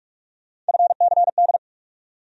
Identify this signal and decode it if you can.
Morse code, heard as a keyed tone, about 42 wpm, 700 Hz. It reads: FCD
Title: USA Call Sign Postfixes - 3 Letters 45wpm